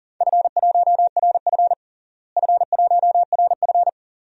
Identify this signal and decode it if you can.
Morse code, heard as a keyed tone, about 40 wpm, 700 Hz. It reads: F1RF F1RF
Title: DXCC-100 Call Signs - 4 Characters - Repeated 2x - 40wpm